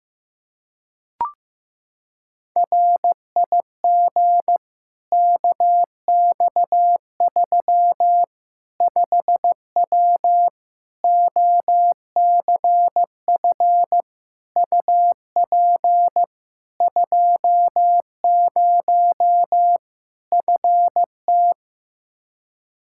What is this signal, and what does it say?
Morse code, heard as a keyed tone, about 15 wpm, 700 Hz. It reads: RIG KX3 5W OCF UP 20 FT
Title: Practice QSO - Exchange 2 - Info - 15wpm